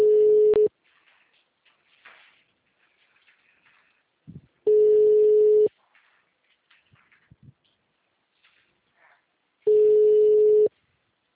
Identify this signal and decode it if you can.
Ringback tone (Europe, 425 Hz).
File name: out- -2013- - - .8640